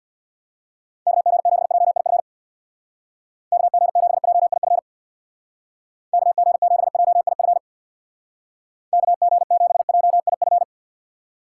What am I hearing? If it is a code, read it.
Morse code, heard as a keyed tone, about 64 wpm, 700 Hz. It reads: KG7JIF KG7JIF KG7JIF KG7JIF